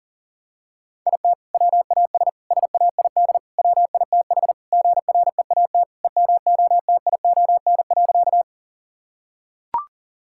Morse code, heard as a keyed tone, about 37 wpm, 700 Hz. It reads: IT WAS SAID WITH GREAT EMOTION.
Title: Sentences from Top 1600 Words 40wpm